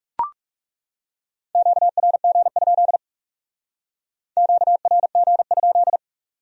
Morse code, heard as a keyed tone, about 38 wpm, 700 Hz. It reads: QRG? QRG?